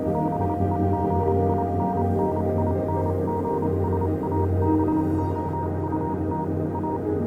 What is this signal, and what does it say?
Morse code, heard as a keyed tone, about 28 wpm, spaced as for 25 wpm, 940 Hz, with noise. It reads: BEGINNINGS ABOUT A
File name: Vylet Pony / aura / do you remember the song she sang (morse code version)